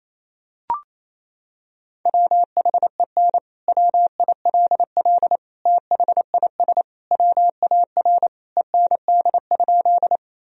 Morse code, heard as a keyed tone, about 28 wpm, 700 Hz. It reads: WHEN WILL THIS WAR END?
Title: Sentences from Top 1000 Words - Review Part 2 - 28wpm